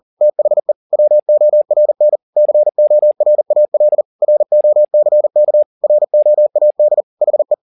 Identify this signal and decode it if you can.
Morse code, heard as a keyed tone, about 40 wpm, 600 Hz. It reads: THE WORN CORAL ROCK ROAD HE
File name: the old man and the sea 2 40wpm